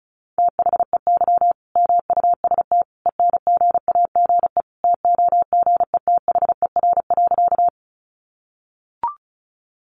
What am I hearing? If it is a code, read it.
Morse code, heard as a keyed tone, about 34 wpm, 700 Hz. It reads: THEY MUST ENGAGE TOGETHER.
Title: Sentences from Top 1500 Words 35wpm